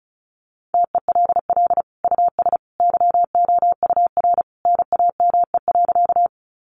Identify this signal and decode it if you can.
Morse code, heard as a keyed tone, about 34 wpm, 700 Hz. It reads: TELL US YOUR NAME.
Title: Sentences from Top 200 Words 35wpm